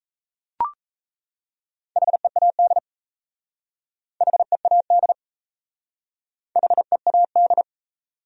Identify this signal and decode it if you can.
Morse code, heard as a keyed tone, about 37 wpm, 700 Hz. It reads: HEAD HEAD HEAD